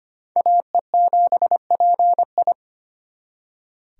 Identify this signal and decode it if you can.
Morse code, heard as a keyed tone, about 31 wpm, 700 Hz. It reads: AE7PI
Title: Speed Racing - USA Call Signs 25wpm